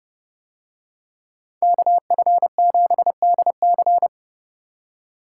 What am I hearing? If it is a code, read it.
Morse code, heard as a keyed tone, about 30 wpm, 700 Hz. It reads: KF7DC